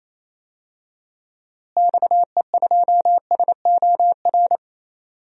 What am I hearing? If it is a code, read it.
Morse code, heard as a keyed tone, about 28 wpm, 700 Hz. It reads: XE2SOR
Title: DXCC-100 Call Signs - 6 Characters - Repeated 2x - 28wpm